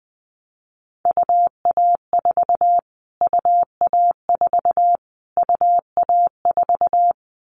Morse code, heard as a keyed tone, about 20 wpm, 700 Hz. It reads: UA4 UA4 UA4
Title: Callsign Prefixes for Top 25 DXCC - Repeated 3x - 20wpm